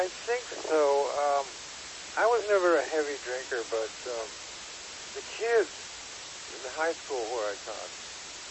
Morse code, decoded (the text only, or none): none